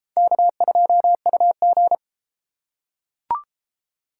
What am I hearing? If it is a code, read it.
Morse code, heard as a keyed tone, about 29 wpm, 700 Hz. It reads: K2UG